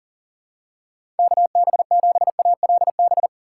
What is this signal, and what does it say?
Morse code, heard as a keyed tone, about 40 wpm, 700 Hz. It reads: KB7ALB
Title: USA Call Signs - 6 Characters - 4x Character Spacing - 40wpm